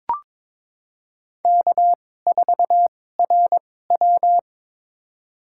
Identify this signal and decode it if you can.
Morse code, heard as a keyed tone, about 22 wpm, spaced as for 16 wpm, 700 Hz. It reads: K4RW
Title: USA Call Signs - 4 Characters - 2x Character Spacing - 22wpm